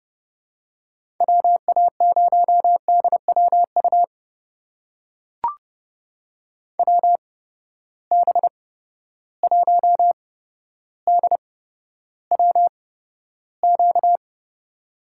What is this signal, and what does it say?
Morse code, heard as a keyed tone, about 30 wpm, spaced as for 13 wpm, 700 Hz. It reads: WA0DWU WB1DWQ
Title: USA Call Signs - 6 Characters - 8x Character Spacing - 30wpm